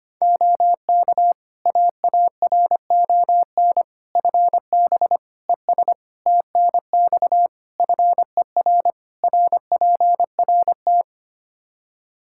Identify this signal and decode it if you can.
Morse code, heard as a keyed tone, about 25 wpm, 700 Hz. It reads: OK AARON FB ES TNX FER RPRT
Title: K1USN SST QSO Practice 25wpm